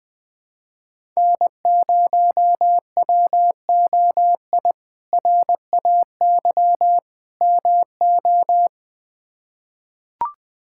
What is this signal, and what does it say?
Morse code, heard as a keyed tone, about 20 wpm, 700 Hz. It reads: N0WOI RAY MO
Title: K1USN SST QSO Practice - 20wpm